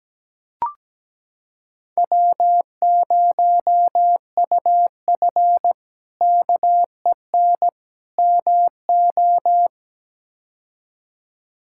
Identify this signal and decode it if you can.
Morse code, heard as a keyed tone, about 17 wpm, 700 Hz. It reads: W0UF KEN MO
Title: K1USN SST QSO Practice - 17wpm